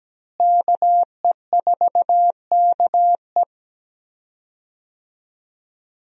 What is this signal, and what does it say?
Morse code, heard as a keyed tone, about 17 wpm, 700 Hz. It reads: KE4KE
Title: USA Call Signs - 5 Characters - Repeated 2x - 17wpm